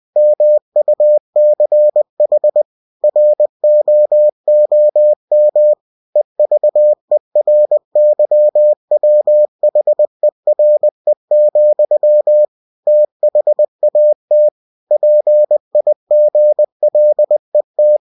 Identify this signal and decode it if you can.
Morse code, heard as a keyed tone, about 20 wpm, 600 Hz. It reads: MUCH ROOM EVERYWHERE, THAT PIGLET
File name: cw-books / winnie-the-pooh / winnie-20wpm-page-0091